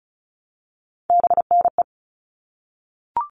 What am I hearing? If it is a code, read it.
Morse code, heard as a keyed tone, about 34 wpm, 700 Hz. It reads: BNE
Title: All Cut Numbers - 3-Digit 35wpm